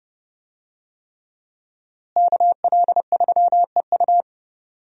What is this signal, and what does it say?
Morse code, heard as a keyed tone, about 30 wpm, 700 Hz. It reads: KL3EU
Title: USA Call Signs - 5 Characters - Repeated 3x - 30wpm